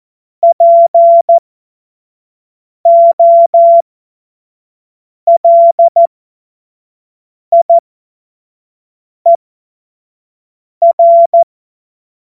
Morse code, heard as a keyed tone, about 14 wpm, spaced as for 5 wpm, 680 Hz. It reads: POLIER
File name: Morsekurs / Witze / Witz 31.Oktober 2023 25 F70